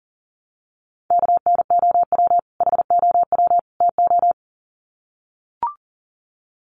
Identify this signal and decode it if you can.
Morse code, heard as a keyed tone, about 40 wpm, 700 Hz. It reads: KNOW HOW TO
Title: Sets of 3 Words 40wpm